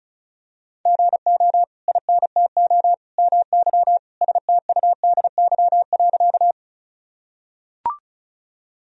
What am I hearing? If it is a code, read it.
Morse code, heard as a keyed tone, about 36 wpm, 700 Hz. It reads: GO INTO MY STUDY.
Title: Sentences from Top 1000 Words - Review Part 1 - 35wpm